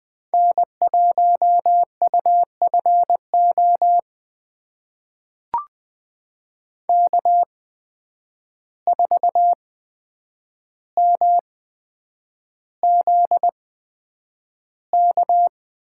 Morse code, heard as a keyed tone, about 20 wpm, spaced as for 9 wpm, 700 Hz. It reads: N1UFO K4MZK
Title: USA Call Signs - 5 Characters - 8x Character Spacing - 20wpm